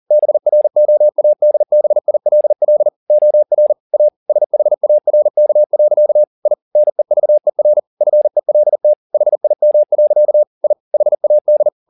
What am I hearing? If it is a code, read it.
Morse code, heard as a keyed tone, about 40 wpm, 600 Hz. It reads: BROADBILL OR A SHARK. I NEVER FELT HIM. I HAD